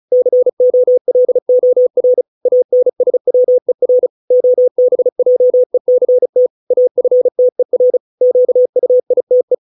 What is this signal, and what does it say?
Morse code, heard as a keyed tone, about 34 wpm, 500 Hz. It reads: COLOR ANSWER OBJECT AFTER QUITE
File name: 632 words / 632 Words 35WPM